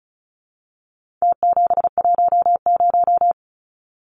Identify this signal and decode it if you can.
Morse code, heard as a keyed tone, about 34 wpm, 700 Hz. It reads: T710
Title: Numbers - Contest Style - 4-Digit 35wpm